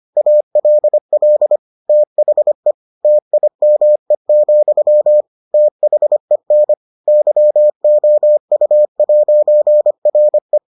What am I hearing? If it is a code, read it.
Morse code, heard as a keyed tone, about 25 wpm, 600 Hz. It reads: ALL THE TIME, THEN YOU'RE